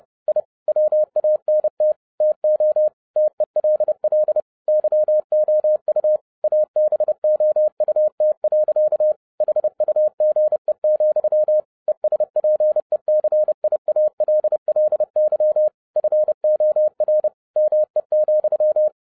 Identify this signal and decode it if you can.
Morse code, heard as a keyed tone, about 30 wpm, 610 Hz. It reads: I WANT TO TELL YOU ABOUT. HUGE, ESPECIALLY FOR ME,
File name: 2011-07-mp3 / n55 BACK FROM A BIG ACCIDENT 30wpm